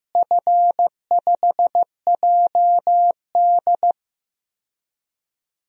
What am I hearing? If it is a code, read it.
Morse code, heard as a keyed tone, about 15 wpm, 700 Hz. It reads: F5JD
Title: DXCC-100 Call Signs - 4 Characters - Repeated 3x - 15wpm